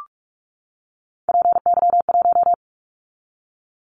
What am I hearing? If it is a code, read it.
Morse code, heard as a keyed tone, about 46 wpm, 700 Hz. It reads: PY1
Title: Callsign Prefixes for Top 50 DXCC - 45wpm